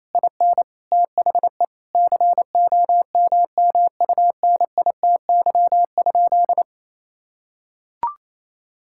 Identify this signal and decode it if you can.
Morse code, heard as a keyed tone, about 28 wpm, 700 Hz. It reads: IN THE COMMUNITY?
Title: Sentences from Top 1000 Words - Review Part 5 - 28wpm